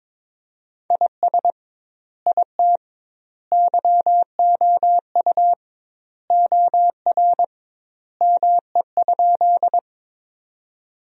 Morse code, heard as a keyed tone, about 22 wpm, 700 Hz. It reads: IS IT YOU OR ME?
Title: 2x Word-Spacing - Sentences from Top 200 Words - 22wpm